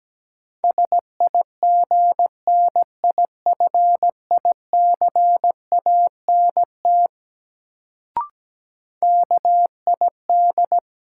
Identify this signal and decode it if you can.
Morse code, heard as a keyed tone, about 17 wpm, 700 Hz. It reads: SIGNIFICANT KID
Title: Sentences from Top 1600 Words - New Word Drill Alt - 17wpm